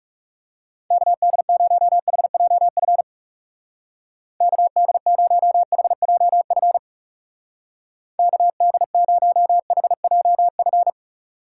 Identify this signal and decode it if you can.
Morse code, heard as a keyed tone, about 36 wpm, 700 Hz. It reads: KD0HJF KD0HJF KD0HJF
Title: Speed Racing - USA 2x3 Call Signs 30wpm